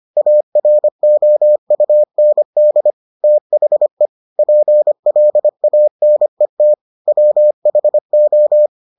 Morse code, heard as a keyed tone, about 25 wpm, 600 Hz. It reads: AROUND THE PLANET WHO